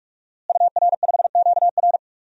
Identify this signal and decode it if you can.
Morse code, heard as a keyed tone, about 44 wpm, 700 Hz. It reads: UR5QR
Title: DXCC-100 Call Signs - 5 Characters - Repeated 3x - 45wpm